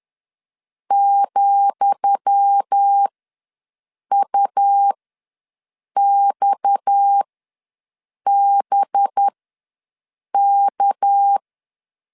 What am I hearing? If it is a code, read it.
Morse code, heard as a keyed tone, about 10 wpm, spaced as for 6 wpm, 800 Hz. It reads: ,UXBK
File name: LEZIONI / 10 SIGNS / 10D COMA ws 10wpm 4314